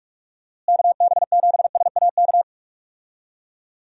Morse code, heard as a keyed tone, about 46 wpm, 700 Hz. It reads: KB7SAK
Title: USA Call Signs - 6 Characters - Repeated 2x - 45wpm